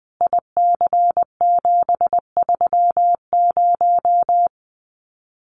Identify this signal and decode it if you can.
Morse code, heard as a keyed tone, about 19 wpm, 700 Hz. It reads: IC730
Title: Common QSO Phrases 20wpm